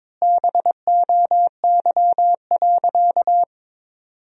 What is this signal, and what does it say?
Morse code, heard as a keyed tone, about 21 wpm, 700 Hz. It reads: BOY.
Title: 3x Word-Spacing - Sentences from Top 200 Words - 22wpm